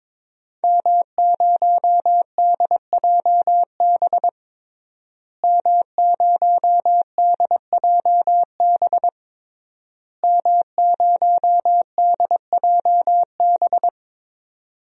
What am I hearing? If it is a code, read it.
Morse code, heard as a keyed tone, about 22 wpm, 700 Hz. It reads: M0DJB M0DJB M0DJB
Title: DXCC-100 Call Signs - 5 Characters - Repeated 3x - 22wpm